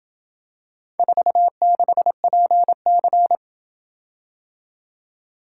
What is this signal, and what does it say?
Morse code, heard as a keyed tone, about 26 wpm, 700 Hz. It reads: 46PC